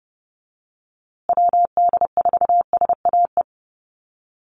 Morse code, heard as a keyed tone, about 30 wpm, 700 Hz. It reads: WD4SAE
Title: USA Call Signs Repeated 3x 30wpm